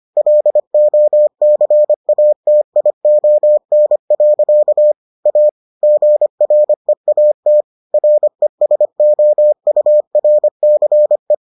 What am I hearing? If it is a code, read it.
Morse code, heard as a keyed tone, about 25 wpm, 600 Hz. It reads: LOCATION. A GREAT RESOURCE